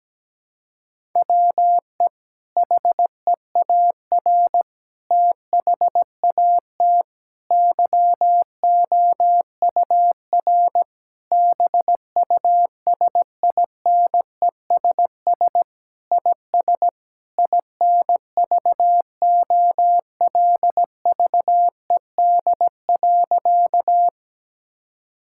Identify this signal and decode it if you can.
Morse code, heard as a keyed tone, about 17 wpm, 700 Hz. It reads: WE HEAR THAT YOUR BUSINESS IS INVOLVED.
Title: Sentences from Top 1000 Words - Review Part 5 - 17wpm